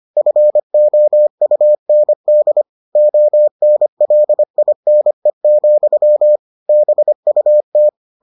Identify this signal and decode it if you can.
Morse code, heard as a keyed tone, about 25 wpm, 600 Hz. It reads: FOUND ONLINE, BUT